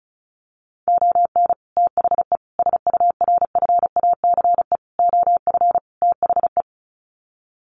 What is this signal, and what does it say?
Morse code, heard as a keyed tone, about 34 wpm, 700 Hz. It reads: ON THE SURFACE OF THE
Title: Sets of 5 Words Encore 35wpm